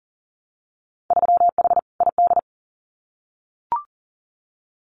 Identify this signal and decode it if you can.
Morse code, heard as a keyed tone, about 40 wpm, 700 Hz. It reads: 3H ID
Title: Winter Field Day Exchanges Encore 40wpm